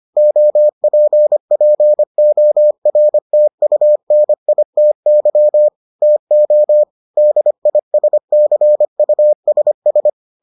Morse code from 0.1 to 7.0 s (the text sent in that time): OPPORTUNITY TO